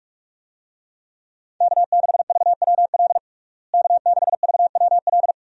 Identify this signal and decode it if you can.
Morse code, heard as a keyed tone, about 46 wpm, 700 Hz. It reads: K6VWL K6VWL
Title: USA Call Signs - 5 Characters - Repeated 2x - 45wpm